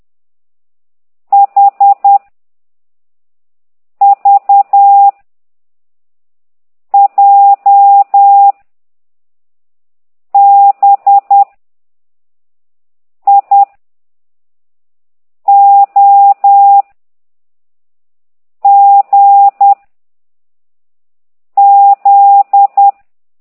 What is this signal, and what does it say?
Morse code, heard as a keyed tone, about 10 wpm, 800 Hz. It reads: H V J B I O G Z
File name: ct2idw / exerc15